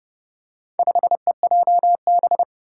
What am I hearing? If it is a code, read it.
Morse code, heard as a keyed tone, about 30 wpm, 700 Hz. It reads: 5EJB
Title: Four Challenging Letter-Number Combinations 30wpm